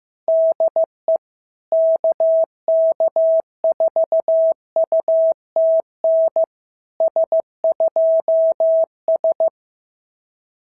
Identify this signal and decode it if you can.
Morse code, heard as a keyed tone, about 16 wpm, 650 Hz. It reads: DE KK4UTN S2S
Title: SOTA QSO Practice 15wpm